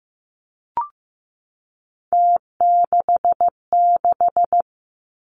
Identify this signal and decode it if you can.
Morse code, heard as a keyed tone, about 15 wpm, 700 Hz. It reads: T66
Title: Three Challenging Letter-Number Combinations 15wpm